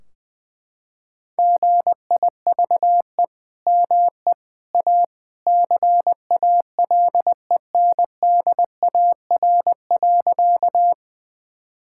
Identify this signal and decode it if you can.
Morse code, heard as a keyed tone, about 20 wpm, 700 Hz. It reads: GIVE ME A CALENDAR.
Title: Sentences from Top 1600 Words 20wpm